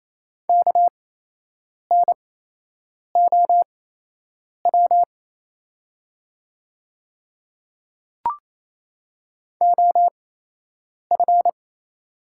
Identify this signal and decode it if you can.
Morse code, heard as a keyed tone, about 27 wpm, spaced as for 8 wpm, 700 Hz. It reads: KNOW OF
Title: 8x Character Spacing - Top 500 Words - 28wpm